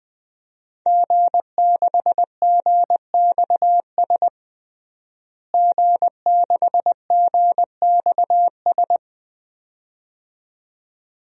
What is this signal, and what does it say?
Morse code, heard as a keyed tone, about 20 wpm, 700 Hz. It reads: G6GXS G6GXS
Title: DXCC-100 Call Signs - 5 Characters - Repeated 2x - 20wpm